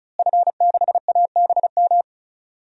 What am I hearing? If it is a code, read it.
Morse code, heard as a keyed tone, about 36 wpm, 700 Hz. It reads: F6ABM
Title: DXCC-100 Call Signs - 5 Characters - Repeated 3x - 35wpm